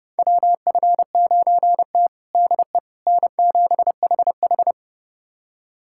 Morse code, heard as a keyed tone, about 30 wpm, 700 Hz. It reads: WF9T DE N7HH
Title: USA Callsign DE Callsign - 4 Characters - 30wpm